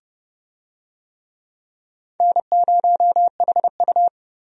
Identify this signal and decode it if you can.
Morse code, heard as a keyed tone, about 31 wpm, 700 Hz. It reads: N0HU
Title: USA Call Signs - 4 Characters - Repeated 3x - 30wpm